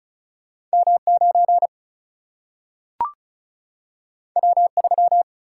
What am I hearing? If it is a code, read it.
Morse code, heard as a keyed tone, about 36 wpm, 700 Hz. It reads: M9 W3
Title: Single-Letter Single-Number 35wpm